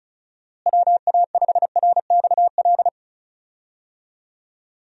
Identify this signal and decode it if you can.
Morse code, heard as a keyed tone, about 36 wpm, 700 Hz. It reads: WA5RXL
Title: USA Call Signs - 6 Characters - Repeated 3x - 35wpm